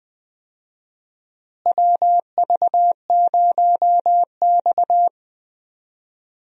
Morse code, heard as a keyed tone, about 20 wpm, 700 Hz. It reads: WV0X
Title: USA Call Signs - 4 Characters - Repeated 2x - 20wpm